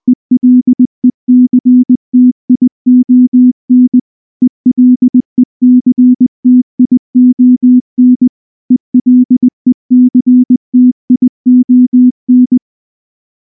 Morse code, heard as a keyed tone, about 20 wpm, 250 Hz. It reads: ELECTION ELECTION ELECTION
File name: morse-training-20WPM